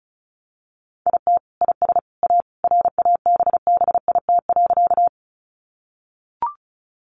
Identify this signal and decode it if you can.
Morse code, heard as a keyed tone, about 34 wpm, 700 Hz. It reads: IT IS A RABBIT.
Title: Easy Phrases 35wpm